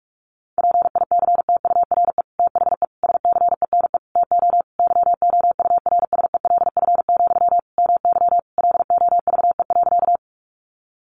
Morse code, heard as a keyed tone, about 44 wpm, 700 Hz. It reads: PICTURE THE SCENE TO YOURSELF, MY LOVE.
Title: Sentences from Top 1100 Words 45wpm